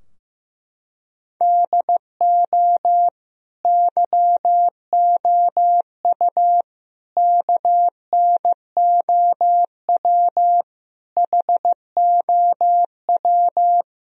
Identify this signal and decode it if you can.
Morse code, heard as a keyed tone, about 16 wpm, 700 Hz. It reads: DO YOU KNOW HOW
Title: Sentences from Top 1000 Words - Review Part 3 - 15wpm